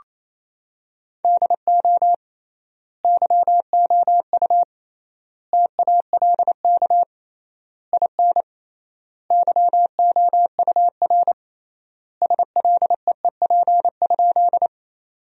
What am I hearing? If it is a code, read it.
Morse code, heard as a keyed tone, about 28 wpm, 700 Hz. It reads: DO YOU TALK IN YOUR SLEEP?
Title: 3x Word-Spacing - Sentences from Top 500 Words - 28wpm